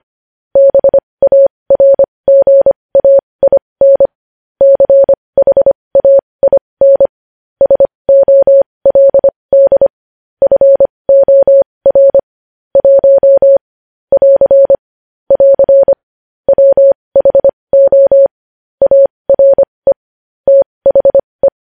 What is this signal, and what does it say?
Morse code, heard as a keyed tone, about 25 wpm, spaced as for 20 wpm, 550 Hz. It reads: BARGAIN CHAIN SOLD FOR 1 <AR> <AR> WHO ARE THE
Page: News headlines in Morse code at 20 wpm